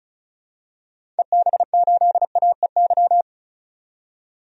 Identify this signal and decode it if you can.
Morse code, heard as a keyed tone, about 36 wpm, 700 Hz. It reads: EB8AEY
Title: DXCC-100 Call Signs - 6 Characters - 35wpm